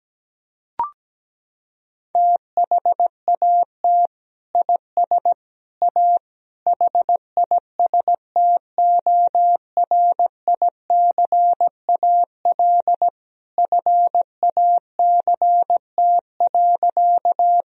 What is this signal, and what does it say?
Morse code, heard as a keyed tone, about 17 wpm, 700 Hz. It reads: THAT IS A HISTORICAL FACT.
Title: Sentences from Top 1000 Words - Review Part 4 - 17wpm